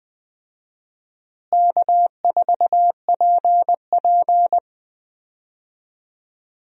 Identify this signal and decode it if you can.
Morse code, heard as a keyed tone, about 20 wpm, 700 Hz. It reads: K4PP
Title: USA Call Signs - 4 Characters - Repeated 3x - 20wpm